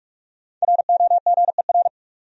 Morse code, heard as a keyed tone, about 46 wpm, 700 Hz. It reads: ROGER
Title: USA Common Names - 45wpm